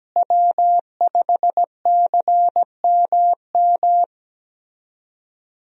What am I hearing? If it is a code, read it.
Morse code, heard as a keyed tone, about 17 wpm, 700 Hz. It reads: W5CMM
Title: USA Call Signs - 5 Characters - Repeated 2x - 17wpm